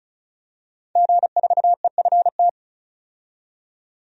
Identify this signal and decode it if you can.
Morse code, heard as a keyed tone, about 36 wpm, 700 Hz. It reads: G4EFT
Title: DXCC-100 Call Signs - 5 Characters - 35wpm